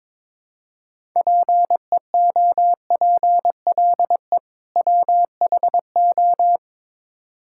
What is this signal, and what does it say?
Morse code, heard as a keyed tone, about 21 wpm, 700 Hz. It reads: PEOPLE WHO
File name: Build A Phrase - 2-Word Phrases - 22wpm